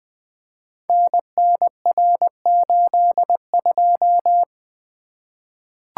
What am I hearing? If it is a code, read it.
Morse code, heard as a keyed tone, about 20 wpm, 700 Hz. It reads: NNR82
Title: Code Groups - 5 Letter:Numbers - 20wpm